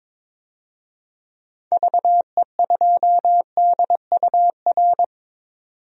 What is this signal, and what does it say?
Morse code, heard as a keyed tone, about 22 wpm, 700 Hz. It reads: VE2DUR
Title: DXCC-100 Call Signs - 6 Characters - 22wpm